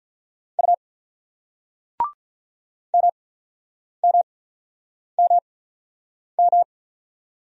Morse code, heard as a keyed tone, about 48 wpm, spaced as for 9 wpm, 700 Hz. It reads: U MMMM